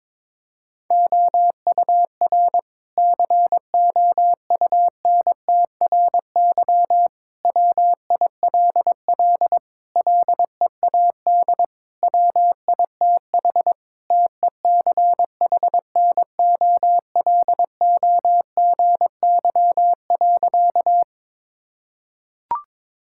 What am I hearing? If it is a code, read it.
Morse code, heard as a keyed tone, about 22 wpm, 700 Hz. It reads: OUR COUNTRY WILL LEAD WITH TECHNOLOGY.
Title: Sentences from Top 1000 Words - Review Part 5 - 22wpm